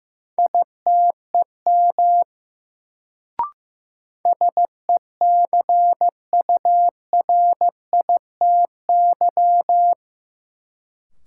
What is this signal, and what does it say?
Morse code, heard as a keyed tone, about 15 wpm, 700 Hz. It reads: ITEM SECURITY
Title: Sentences from Top 1400 Words - New Word Drill Alt - 15wpm